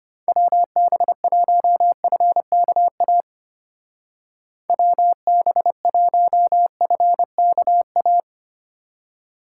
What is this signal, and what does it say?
Morse code, heard as a keyed tone, about 33 wpm, 700 Hz. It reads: WB1FKA WB1FKA